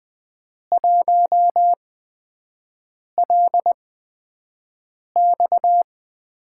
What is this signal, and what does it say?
Morse code, heard as a keyed tone, about 20 wpm, spaced as for 5 wpm, 700 Hz. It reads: 1LX
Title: USA Call Signs - 5 Characters - 8x Character Spacing - 20wpm